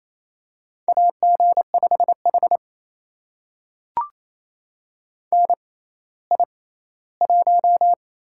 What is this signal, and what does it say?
Morse code, heard as a keyed tone, about 27 wpm, 700 Hz. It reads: A G 5 H N I 1
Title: USA Call Signs - 4 Characters - 6x Character Spacing - 28wpm